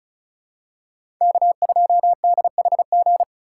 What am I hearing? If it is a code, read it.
Morse code, heard as a keyed tone, about 36 wpm, 700 Hz. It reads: K2DHG
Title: USA Call Signs - 5 Characters - 2x Character Spacing - 35wpm